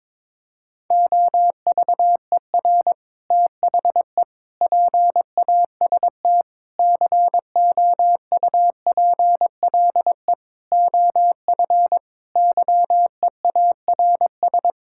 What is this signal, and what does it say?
Morse code, heard as a keyed tone, about 22 wpm, 700 Hz. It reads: OVER THE PAST COUPLE OF YEARS
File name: practice / 22wpm / Sets of 6 Words - 22wpm